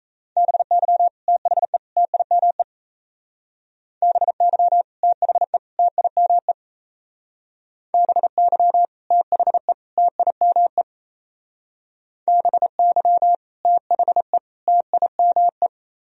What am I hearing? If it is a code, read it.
Morse code, heard as a keyed tone, about 34 wpm, 700 Hz. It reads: BY THE TIME BY THE TIME BY THE TIME BY THE TIME
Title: Speed-Racing - Sets of 3 Words 28wpm